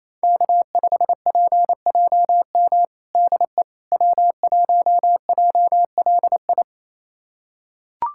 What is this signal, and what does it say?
Morse code, heard as a keyed tone, about 28 wpm, 700 Hz. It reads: K5PJM DE W1JLI
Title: USA Callsign DE Callsign - 5 Characters - 28wpm